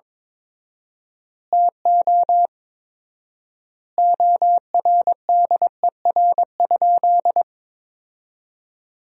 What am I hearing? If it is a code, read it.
Morse code, heard as a keyed tone, about 22 wpm, 700 Hz. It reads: TO ORDER?
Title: 4x Word-Spacing - Sentences from Top 400 Words - 22wpm